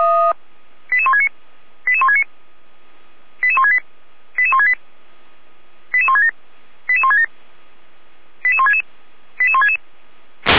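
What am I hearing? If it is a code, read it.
Five-tone selective call (ZVEI), heard as quick runs of five tones, tones of about 70 ms. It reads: 88169 88169 88168 88168 88167 88167 88166 88166
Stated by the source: ZVEI five-tone selective call received with an SDR.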